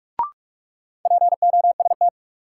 Morse code, heard as a keyed tone, about 41 wpm, 700 Hz. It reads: POST
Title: Sentences from Top 1000 Words - New Word Drill - 45wpm